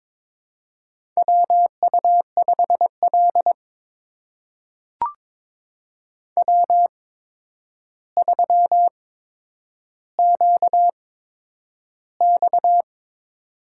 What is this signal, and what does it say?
Morse code, heard as a keyed tone, about 22 wpm, spaced as for 9 wpm, 700 Hz. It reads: WU5L W3QX
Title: USA Call Signs - 4 Characters - 8x Character Spacing - 22wpm